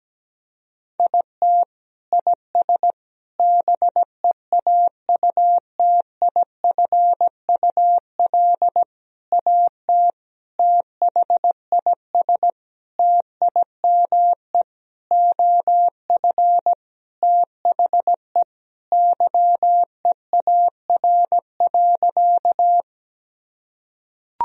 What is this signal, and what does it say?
Morse code, heard as a keyed tone, about 17 wpm, 700 Hz. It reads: IT IS BEAUTIFUL AT THIS TIME OF THE YEAR.
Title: Sentences from Top 1000 Words - Review Part 1 - 17wpm